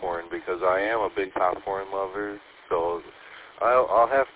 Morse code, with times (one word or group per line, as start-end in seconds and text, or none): none